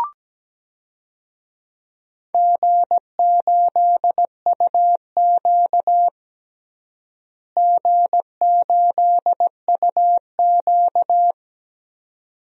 Morse code, heard as a keyed tone, about 17 wpm, 700 Hz. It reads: G8UQ G8UQ
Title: DXCC-100 Call Signs - 4 Characters - Repeated 2x - 17wpm